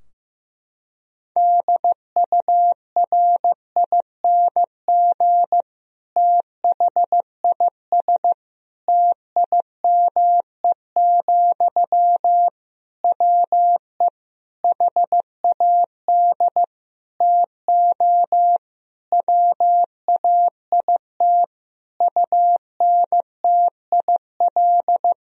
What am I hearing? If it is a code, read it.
Morse code, heard as a keyed tone, about 15 wpm, 700 Hz. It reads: DURING THIS TIME, WE HAD TO WAIT UNTIL
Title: Sentences from Top 1000 Words - Review Part 2 - 15wpm